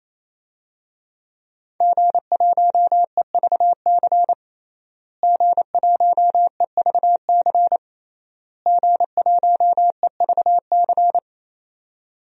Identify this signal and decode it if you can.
Morse code, heard as a keyed tone, about 28 wpm, 700 Hz. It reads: G1EVC G1EVC G1EVC
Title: DXCC-100 Call Signs - 5 Characters - Repeated 3x - 28wpm